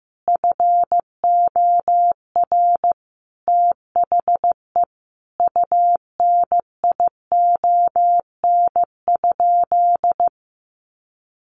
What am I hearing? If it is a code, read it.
Morse code, heard as a keyed tone, about 15 wpm, 700 Hz. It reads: FOR THE UNION?
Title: Sentences from Top 1400 Words 15wpm